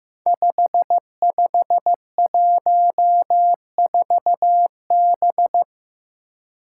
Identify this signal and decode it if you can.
Morse code, heard as a keyed tone, about 15 wpm, 700 Hz. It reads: 5514B
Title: Five Challenging Letter-Number Combinations 15wpm